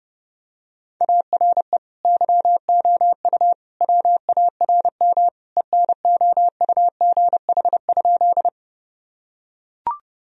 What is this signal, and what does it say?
Morse code, heard as a keyed tone, about 32 wpm, 700 Hz. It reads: ARE YOU WARM ENOUGH?
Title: Sentences from Top 1000 Words - Review Part 2 - 30wpm